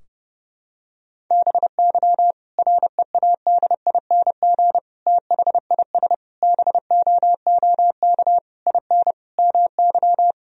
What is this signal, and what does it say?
Morse code, heard as a keyed tone, about 30 wpm, 700 Hz. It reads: BY READING THIS BOOK IN MY
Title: Sentences from Top 1000 Words - Review Part 3 - 30wpm